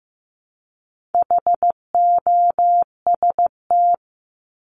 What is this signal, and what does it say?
Morse code, heard as a keyed tone, about 15 wpm, 700 Hz. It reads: HOST
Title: Top 1501-1750 Words 15wpm